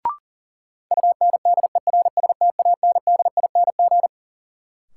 Morse code, heard as a keyed tone, about 37 wpm, 700 Hz. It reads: UNDERSTANDING
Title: Sentences from Top 1000 Words - Review Part 3 - New Word Drill - 40wpm